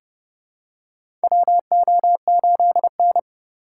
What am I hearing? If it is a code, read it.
Morse code, heard as a keyed tone, about 30 wpm, 700 Hz. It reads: WO8N
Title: USA Call Signs - 4 Characters - 3x Character Spacing - 30wpm